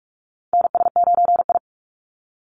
Morse code, heard as a keyed tone, about 46 wpm, 700 Hz. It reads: NS9I
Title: Call Signs - 45wpm